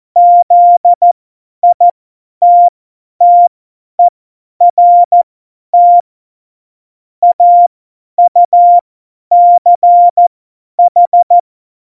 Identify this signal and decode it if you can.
Morse code, heard as a keyed tone, about 14 wpm, spaced as for 10 wpm, 700 Hz. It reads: ZITTERT AUCH
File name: Morsekurs / Witze / Witz 14 Juli 2020, 50 F70